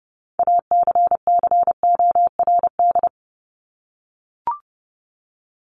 Morse code, heard as a keyed tone, about 30 wpm, 700 Hz. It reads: ACCORD
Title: ING Suffix - Top 100 Words with Base Words 30wpm